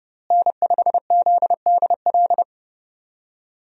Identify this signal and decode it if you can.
Morse code, heard as a keyed tone, about 31 wpm, 700 Hz. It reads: N5ZDL
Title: USA Call Signs - 5 Characters - Repeated 3x - 30wpm